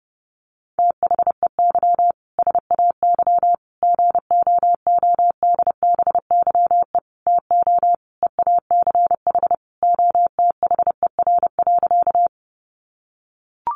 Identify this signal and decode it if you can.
Morse code, heard as a keyed tone, about 30 wpm, 700 Hz. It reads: THEY SAY GOODBYE TO EACH OTHER.
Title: Additional Easy Phrases 30wpm